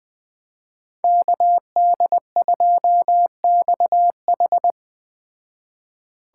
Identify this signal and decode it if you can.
Morse code, heard as a keyed tone, about 20 wpm, 700 Hz. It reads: KD2XH
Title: Speed Racing - USA Call Signs 17wpm